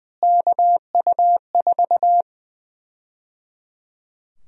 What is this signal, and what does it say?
Morse code, heard as a keyed tone, about 20 wpm, 700 Hz. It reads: KU4
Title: USA Call Sign Prefixes - Repeated 2x - 20wpm